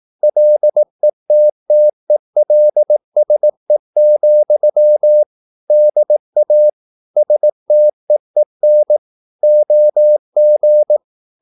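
Morse code, heard as a keyed tone, about 18 wpm, 600 Hz. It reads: LETTELSE, DA STEEN OG
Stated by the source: Slow (QRS) Morse practice at 18 wpm, CW med Gnister.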